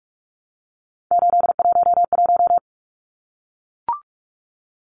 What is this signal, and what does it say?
Morse code, heard as a keyed tone, about 44 wpm, 700 Hz. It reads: 811
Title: Numbers - 3-Digit 45wpm